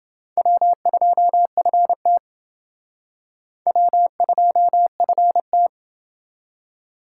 Morse code, heard as a keyed tone, about 28 wpm, 700 Hz. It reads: W2FT W2FT